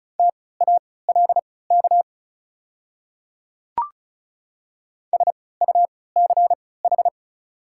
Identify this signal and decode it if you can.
Morse code, heard as a keyed tone, about 34 wpm, spaced as for 20 wpm, 700 Hz. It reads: TALK SUCH
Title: 3x Character Spacing - Top 500 Words - 35wpm